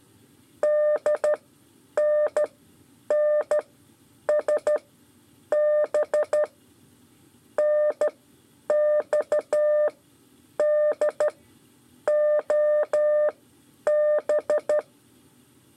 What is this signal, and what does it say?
Morse code, heard as a keyed tone, about 12 wpm, spaced as for 8 wpm, 600 Hz. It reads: DNNSB NXDOB
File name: 5 LEZIONE RIPASSO 2